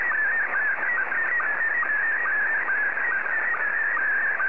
SSTV (slow-scan television), Scottie 1.
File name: sstv / S1-ZL2TKM